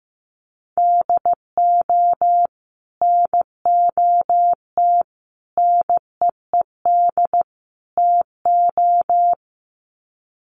Morse code, heard as a keyed tone, about 15 wpm, 700 Hz. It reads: DO NOT NEED TO
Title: Sets of 4 Words 15wpm